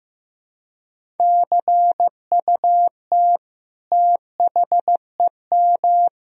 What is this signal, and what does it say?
Morse code, heard as a keyed tone, about 16 wpm, 700 Hz. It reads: CUT THEM
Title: Sentences from Top 1000 Words - Review Part 1 - 15wpm